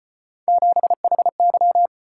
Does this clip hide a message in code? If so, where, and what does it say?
Yes: Morse code from 0.5 to 1.9 s, heard as a keyed tone, about 32 wpm, 700 Hz. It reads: 7HY